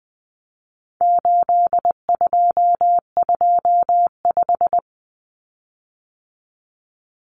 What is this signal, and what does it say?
Morse code, heard as a keyed tone, about 20 wpm, 700 Hz. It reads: 8225
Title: Common Cut Numbers - 4-Digit 20wpm